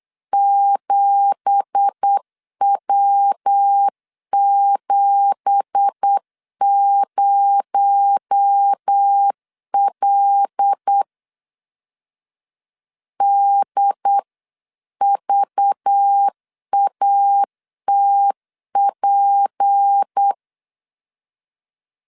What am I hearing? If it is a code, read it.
Morse code, heard as a keyed tone, about 8 wpm, 800 Hz. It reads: 7W70L DVATP
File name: LEZIONI / 9 2867 8wpm 3440